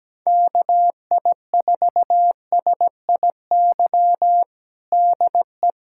Morse code, heard as a keyed tone, about 17 wpm, 700 Hz. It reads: KI4SIY DE
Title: USA Callsign DE Callsign 17wpm